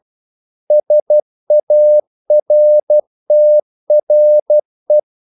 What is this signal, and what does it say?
Morse code, heard as a keyed tone, about 12 wpm, 600 Hz. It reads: SARTRE
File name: CW / Practice / EnglishQuotations0312